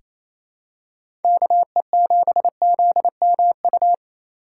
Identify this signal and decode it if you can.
Morse code, heard as a keyed tone, about 28 wpm, 700 Hz. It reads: KE7ZMU
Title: USA Call Signs - 3x Character Spacing - 28wpm